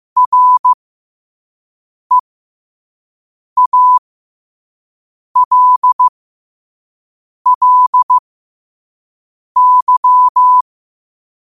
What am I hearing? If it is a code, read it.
Morse code, heard as a keyed tone, about 15 wpm, spaced as for 5 wpm, 1000 Hz. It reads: REALLY
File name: qotdfiles / 5wpm